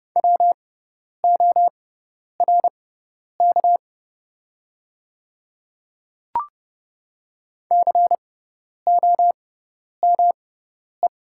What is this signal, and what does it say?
Morse code, heard as a keyed tone, about 30 wpm, spaced as for 10 wpm, 700 Hz. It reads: WORK COME
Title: 6x Character Spacing - Top 200 Words - 30wpm